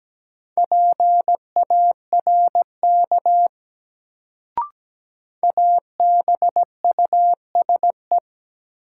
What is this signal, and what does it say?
Morse code, heard as a keyed tone, about 17 wpm, 700 Hz. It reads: PARK ABUSE
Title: Sentences from Top 1300 Words - New Word Drill - 17wpm